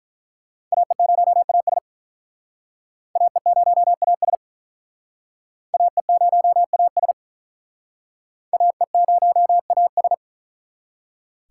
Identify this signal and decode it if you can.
Morse code, heard as a keyed tone, about 42 wpm, 700 Hz. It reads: AE0AS AE0AS AE0AS AE0AS
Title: Speed Racing - USA 2x2 Call Signs 35wpm